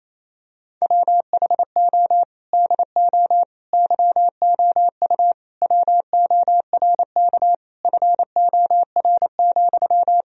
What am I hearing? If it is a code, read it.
Morse code, heard as a keyed tone, about 28 wpm, 700 Hz. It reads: WHO DO YOU WORK FOR,
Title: Sentences from Top 1000 Words - Review Part 4 - 28wpm